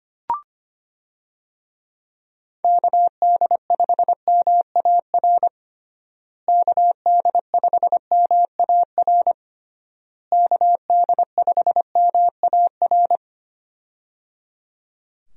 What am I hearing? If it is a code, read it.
Morse code, heard as a keyed tone, about 25 wpm, 700 Hz. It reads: KD5MAR KD5MAR KD5MAR
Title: USA Call Signs - 6 Characters - Repeated 3x - 25wpm